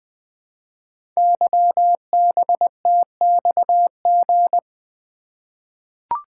Morse code, heard as a keyed tone, about 20 wpm, 700 Hz. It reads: YBTXG
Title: Code Groups - 5 Letters - 20wpm